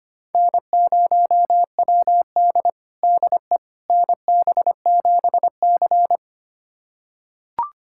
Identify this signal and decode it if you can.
Morse code, heard as a keyed tone, about 25 wpm, 700 Hz. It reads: N0WD DE NB7C
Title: USA Callsign DE Callsign - 4 Characters - 25wpm